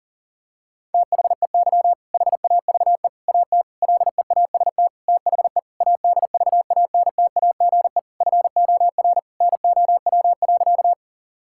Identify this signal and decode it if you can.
Morse code, heard as a keyed tone, about 40 wpm, 700 Hz. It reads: THEY HAVE AT LEAST THE ADVANTAGE FOR NOW.
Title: Sentences from Top 1000 Words - Review Part 6 - 40wpm